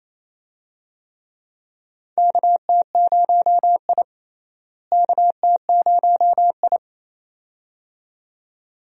Morse code, heard as a keyed tone, about 28 wpm, 700 Hz. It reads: KT0I KT0I
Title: USA Call Signs - 4 Characters - Repeated 2x - 28wpm